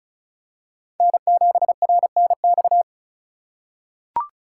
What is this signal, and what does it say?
Morse code, heard as a keyed tone, about 36 wpm, 700 Hz. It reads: N7RNX
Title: USA Call Signs - 5 Characters - 4x Character Spacing - 35wpm